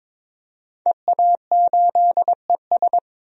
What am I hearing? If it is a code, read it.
Morse code, heard as a keyed tone, about 21 wpm, 700 Hz. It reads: EA8ES
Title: DXCC-100 Call Signs - 5 Characters - 22wpm